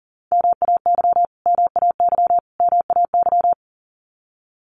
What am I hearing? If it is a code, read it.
Morse code, heard as a keyed tone, about 40 wpm, 700 Hz. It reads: MAY MAY MAY
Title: Top 200 Words Repeated 40wpm